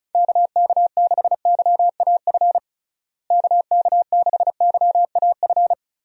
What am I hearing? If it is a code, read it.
Morse code, heard as a keyed tone, about 34 wpm, 700 Hz. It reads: KK6YAF KK6YAF
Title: USA Call Signs - 6 Characters - Repeated 3x - 35wpm